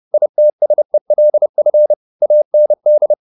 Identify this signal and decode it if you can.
Morse code, heard as a keyed tone, about 30 wpm, 600 Hz. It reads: ITSELF AND
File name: the old man and the sea / the old man and the sea 1 30wpm